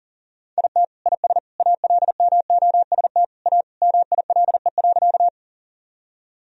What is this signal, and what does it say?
Morse code, heard as a keyed tone, about 40 wpm, 700 Hz. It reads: IT IS ALMOST A MILE.
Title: Sentences from Top 1000 Words - Review Part 4 - 40wpm